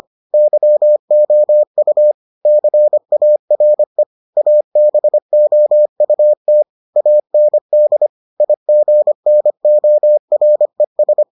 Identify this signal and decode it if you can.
Morse code, heard as a keyed tone, about 25 wpm, 600 Hz. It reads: YOU CARE ABOUT AND IGNORES